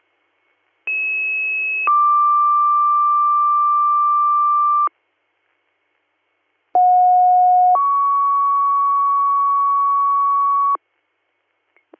Two-tone page, heard as a one-second tone, then a three-second tone, about 2600 then 1180 Hz, about 730 then 1120 Hz. It is heard from 0.9 to 10.8 s.